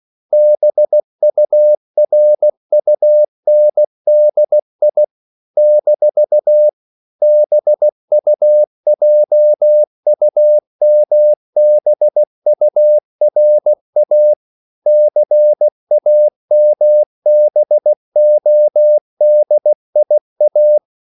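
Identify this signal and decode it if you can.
Morse code, heard as a keyed tone, about 16 wpm, 600 Hz. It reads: BURUNDI - BUJUMBURA CAMBODIA
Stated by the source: CW practice of national capitals at 16 words per minute, from ON6ZQ.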